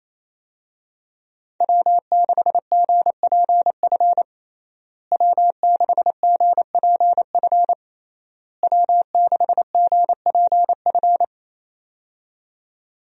Morse code, heard as a keyed tone, about 27 wpm, 700 Hz. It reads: W6GPF W6GPF W6GPF
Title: USA Call Signs - 5 Characters - Repeated 3x - 28wpm